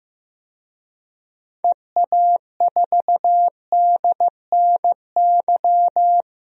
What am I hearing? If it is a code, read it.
Morse code, heard as a keyed tone, about 15 wpm, 700 Hz. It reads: EA4DNY
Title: DXCC-100 Call Signs - 6 Characters - 15wpm